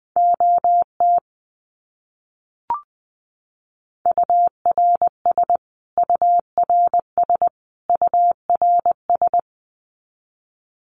Morse code, heard as a keyed tone, about 20 wpm, 700 Hz. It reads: OT URS URS URS
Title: Abbreviations Repeated 3x - 20wpm